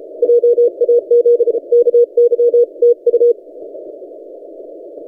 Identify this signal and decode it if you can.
Morse code, heard as a keyed tone, about 32 wpm, 480 Hz. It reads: JAZ KY TU
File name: posts / vk9ec-40cw- -1410z